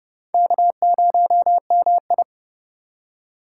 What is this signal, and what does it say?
Morse code, heard as a keyed tone, about 30 wpm, 700 Hz. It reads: K0MI